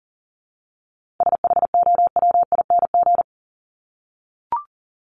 Morse code, heard as a keyed tone, about 40 wpm, 700 Hz. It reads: SHOWING
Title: ING Suffix - Top 100 Words with Base Words 40wpm